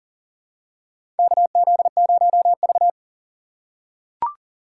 Morse code, heard as a keyed tone, about 40 wpm, 700 Hz. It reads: KZ0V
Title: USA Call Signs - 4 Characters - 2x Character Spacing - 40wpm